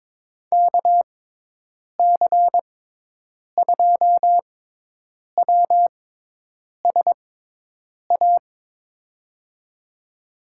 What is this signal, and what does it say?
Morse code, heard as a keyed tone, about 23 wpm, spaced as for 8 wpm, 700 Hz. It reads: KC2WSA
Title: USA Call Signs - 6x Character Spacing - 22wpm